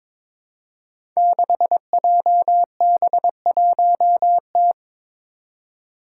Morse code, heard as a keyed tone, about 22 wpm, 700 Hz. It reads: 6JB1T
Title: Five Challenging Letter-Number Combinations 22wpm